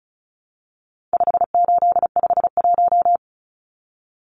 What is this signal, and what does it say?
Morse code, heard as a keyed tone, about 34 wpm, 700 Hz. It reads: 5851
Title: Numbers - 4-Digit 35wpm